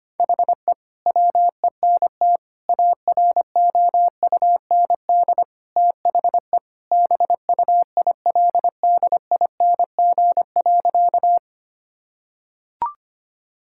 Morse code, heard as a keyed tone, about 25 wpm, 700 Hz. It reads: HE WENT AROUND THE BUILDING.
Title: Sentences from Top 1000 Words - Review Part 5 - 25wpm